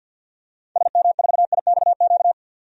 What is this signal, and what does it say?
Morse code, heard as a keyed tone, about 46 wpm, 700 Hz. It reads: SM4IXQ
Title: DXCC-100 Call Signs - 6 Characters - 50wpm